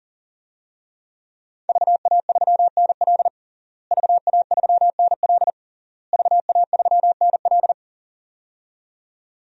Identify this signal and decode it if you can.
Morse code, heard as a keyed tone, about 40 wpm, 700 Hz. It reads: VA3NL VA3NL VA3NL
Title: DXCC-100 Call Signs - 5 Characters - Repeated 3x - 40wpm